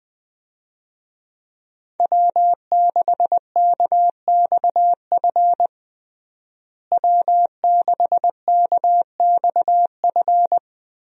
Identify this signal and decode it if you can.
Morse code, heard as a keyed tone, about 21 wpm, 700 Hz. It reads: W6KXF W6KXF
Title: USA Call Signs - 5 Characters - Repeated 2x - 20wpm